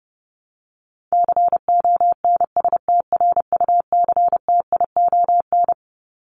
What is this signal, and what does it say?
Morse code, heard as a keyed tone, about 28 wpm, 700 Hz. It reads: CONSTRUCTION
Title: Top 1501-1750 Words 30wpm